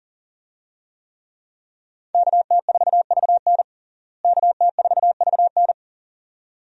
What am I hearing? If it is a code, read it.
Morse code, heard as a keyed tone, about 40 wpm, 700 Hz. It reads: KT4VN KT4VN
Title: USA Call Signs - 5 Characters - Repeated 2x - 40wpm